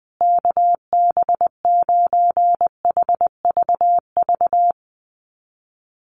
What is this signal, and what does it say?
Morse code, heard as a keyed tone, about 20 wpm, 700 Hz. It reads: KB9HVV
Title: USA Call Signs Repeated 3x 20wpm